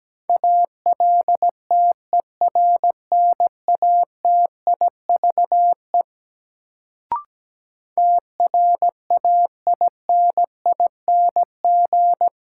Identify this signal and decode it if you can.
Morse code, heard as a keyed tone, about 17 wpm, 700 Hz. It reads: ALTERNATIVE TRAINING
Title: Sentences from Top 1000 Words - Review Part 4 - New Word Drill - 17wpm